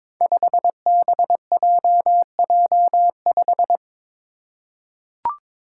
Morse code, heard as a keyed tone, about 22 wpm, 700 Hz. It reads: 5BJJ5
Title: Five Challenging Letter-Number Combinations 22wpm